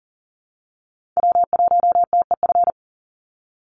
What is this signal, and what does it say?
Morse code, heard as a keyed tone, about 40 wpm, 700 Hz. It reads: W1TEF
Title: Call Signs Repeated 2x - 40wpm